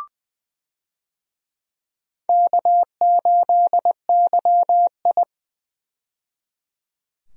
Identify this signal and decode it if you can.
Morse code, heard as a keyed tone, about 20 wpm, 700 Hz. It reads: K8YI
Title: USA Call Signs - 4 Characters - 20wpm